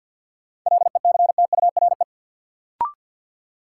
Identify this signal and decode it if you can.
Morse code, heard as a keyed tone, about 48 wpm, 700 Hz. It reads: LECTURE
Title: Sentences from Top 1600 Words - New Word Drill Alt - 50wpm